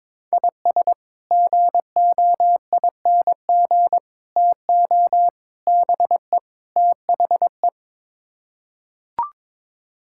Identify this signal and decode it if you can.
Morse code, heard as a keyed tone, about 21 wpm, 700 Hz. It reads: IS GOING TO BE THE
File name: practice / 22wpm / Build A Phrase - 5-Word Phrases - 22wpm